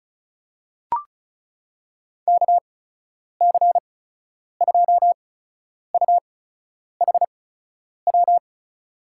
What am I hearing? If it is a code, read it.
Morse code, heard as a keyed tone, about 36 wpm, spaced as for 10 wpm, 700 Hz. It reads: KC2UHW
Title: USA Call Signs - 8x Character Spacing - 35wpm